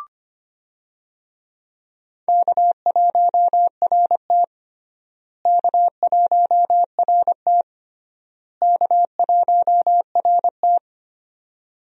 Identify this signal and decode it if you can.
Morse code, heard as a keyed tone, about 25 wpm, 700 Hz. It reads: K1RT K1RT K1RT
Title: USA Call Signs - 4 Characters - Repeated 3x - 25wpm